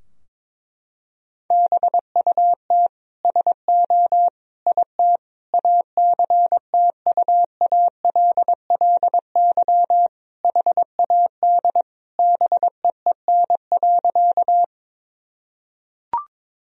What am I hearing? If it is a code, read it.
Morse code, heard as a keyed tone, about 22 wpm, 700 Hz. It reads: BUT SO IT ACTUALLY HAD BEEN.
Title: Sentences from Top 1000 Words - Review Part 4 - 22wpm